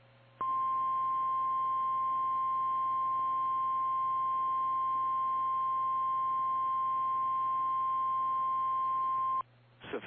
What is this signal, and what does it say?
Emergency Alert System (SAME) tones (the 1050 Hz attention tone). It is heard from 0.4 to 9.4 s.
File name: eas / KKPRKXPN / KKPRKXPN